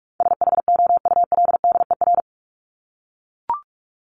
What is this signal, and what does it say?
Morse code, heard as a keyed tone, about 44 wpm, 700 Hz. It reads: SHOULDER
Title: Top 1501-1750 Words 45wpm